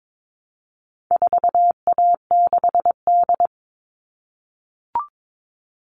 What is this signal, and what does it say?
Morse code, heard as a keyed tone, about 22 wpm, 700 Hz. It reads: 4A6D
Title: All Cut Numbers - 4-Digit 22wpm